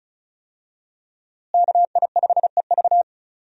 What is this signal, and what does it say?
Morse code, heard as a keyed tone, about 34 wpm, 700 Hz. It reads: KI5EV
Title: USA Call Signs - 5 Characters - Repeated 3x - 35wpm